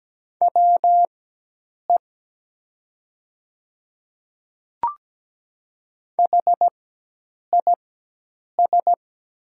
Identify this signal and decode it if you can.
Morse code, heard as a keyed tone, about 17 wpm, spaced as for 8 wpm, 700 Hz. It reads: WE HIS
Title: 4x Character Spacing - Top 200 Words - 17wpm